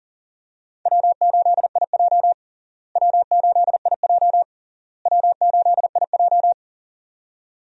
Morse code, heard as a keyed tone, about 40 wpm, 700 Hz. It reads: W8IJ W8IJ W8IJ
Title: USA Call Signs - 4 Characters - Repeated 3x - 40wpm